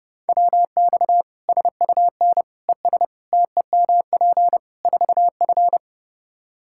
Morse code, heard as a keyed tone, about 30 wpm, 700 Hz. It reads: WX SUN ES TEMP 4F
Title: Practice QSO - Exchange 2 - Info - 30wpm